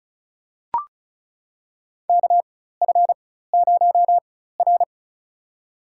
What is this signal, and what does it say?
Morse code, heard as a keyed tone, about 34 wpm, spaced as for 16 wpm, 700 Hz. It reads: KF0R
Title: USA Call Signs - 4 Characters - 4x Character Spacing - 35wpm